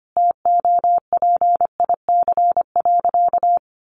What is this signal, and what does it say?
Morse code, heard as a keyed tone, about 25 wpm, 700 Hz. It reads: TOPIC.
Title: Sentences from Top 1200 Words 25wpm